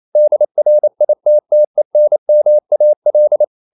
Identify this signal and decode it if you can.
Morse code, heard as a keyed tone, about 28 wpm, 600 Hz. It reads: DRITTENMAL
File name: marienkind9 28wpm